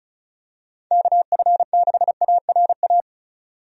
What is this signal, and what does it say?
Morse code, heard as a keyed tone, about 36 wpm, 700 Hz. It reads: KF6ARA
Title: USA Call Signs - 6 Characters - 35wpm